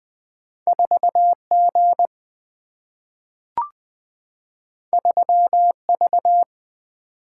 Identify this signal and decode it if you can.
Morse code, heard as a keyed tone, about 20 wpm, 700 Hz. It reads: 4 G 3 V
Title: Single-Letter Single-Number 20wpm